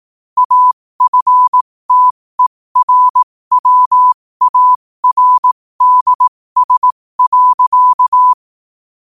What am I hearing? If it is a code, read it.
Morse code, heard as a keyed tone, about 18 wpm, spaced as for 15 wpm, 1000 Hz. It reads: AFTERWARDS.
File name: qotdfiles / 15wpm